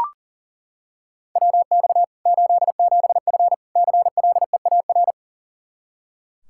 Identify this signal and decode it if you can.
Morse code, heard as a keyed tone, about 37 wpm, 700 Hz. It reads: WX 87F CLEAR
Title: Practice QSO - Exchange 2 - Info - 40wpm